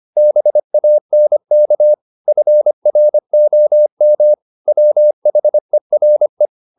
Morse code, heard as a keyed tone, about 25 wpm, 600 Hz. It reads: BANK FROM WHERE
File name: the old man and the sea part 1 25wpm